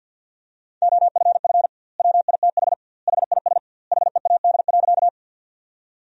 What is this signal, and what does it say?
Morse code, heard as a keyed tone, about 50 wpm, 700 Hz. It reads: OFF WITH HIS HEAD.
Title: 2x Word-Spacing - Sentences from Top 300 Words - 50wpm